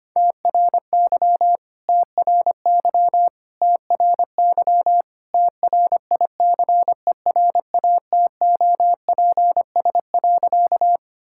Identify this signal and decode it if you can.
Morse code, heard as a keyed tone, about 25 wpm, 700 Hz. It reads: TRY TRY TRY TRICERATOPS.
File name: practice / 25wpm / All-Age Appropriate Jokes - 25wpm